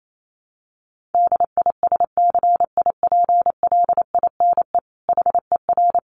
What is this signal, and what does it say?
Morse code, heard as a keyed tone, about 28 wpm, 700 Hz. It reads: DISCIPLINE HER
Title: Sentences from Top 900 Words 28wpm